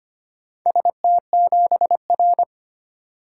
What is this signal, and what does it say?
Morse code, heard as a keyed tone, about 25 wpm, 700 Hz. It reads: ST7R
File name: practice / 25wpm / Four Letter-Number Combinations - 25wpm